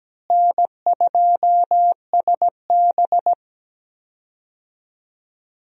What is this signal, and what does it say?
Morse code, heard as a keyed tone, about 17 wpm, 700 Hz. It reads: N2SB
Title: USA Call Signs - 4 Characters - Repeated 2x - 17wpm